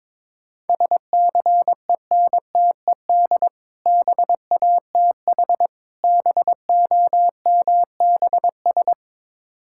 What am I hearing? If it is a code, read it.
Morse code, heard as a keyed tone, about 22 wpm, 700 Hz. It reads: SCENTED BATH BOMBS
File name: Difficult Words with Hints - 22wpm